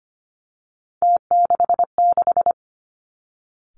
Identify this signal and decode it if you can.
Morse code, heard as a keyed tone, about 25 wpm, 700 Hz. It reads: T66
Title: Three Challenging Letter-Number Combinations 25wpm